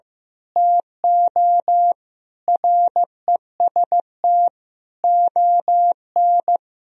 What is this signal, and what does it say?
Morse code, heard as a keyed tone, about 16 wpm, 700 Hz. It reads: TO REST ON
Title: Sentences from Top 1000 Words - Review Part 3 - 15wpm